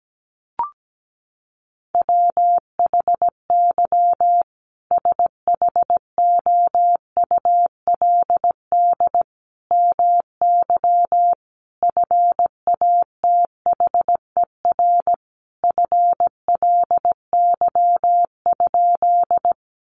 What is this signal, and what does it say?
Morse code, heard as a keyed tone, about 17 wpm, 700 Hz. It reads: WHY SHOULD MY FATHER FLY?
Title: Sentences from Top 500 Words 17wpm